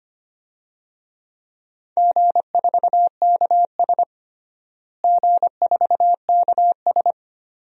Morse code, heard as a keyed tone, about 25 wpm, 700 Hz. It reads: G4KS G4KS
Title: DXCC-100 Call Signs - 4 Characters - Repeated 2x - 25wpm